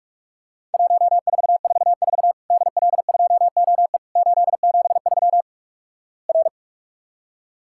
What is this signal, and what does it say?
Morse code, heard as a keyed tone, about 48 wpm, 700 Hz. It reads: 1444 DL2OE 873 R
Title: Worked All Europe DX Contest - QTC Exchange - 45wpm